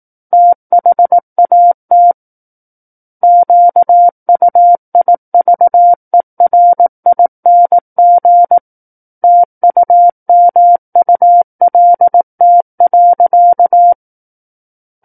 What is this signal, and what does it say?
Morse code, heard as a keyed tone, about 18 wpm, 700 Hz. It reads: THAT QUIVERING TUMULT.
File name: Book One / 09-Chapter 9 - 18 WPM